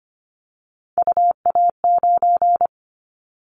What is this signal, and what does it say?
Morse code, heard as a keyed tone, about 25 wpm, 700 Hz. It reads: UA9
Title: Callsign Prefixes for Top 25 DXCC - 25wpm